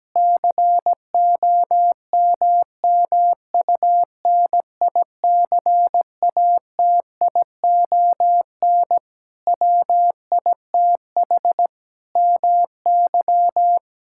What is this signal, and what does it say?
Morse code, heard as a keyed tone, about 17 wpm, 700 Hz. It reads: COMMUNICATION WITH MY
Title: Sentences from Top 1000 Words - Review Part 6 - 17wpm